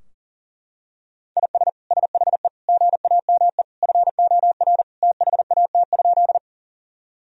Morse code, heard as a keyed tone, about 37 wpm, 700 Hz. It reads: IS SHE GAME FOR THAT?
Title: Sentences from Top 1000 Words - Review Part 3 - 40wpm